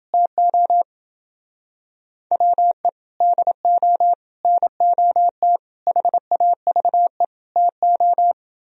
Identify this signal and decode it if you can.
Morse code, heard as a keyed tone, about 30 wpm, 700 Hz. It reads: TO WE DO NOT HAVE TO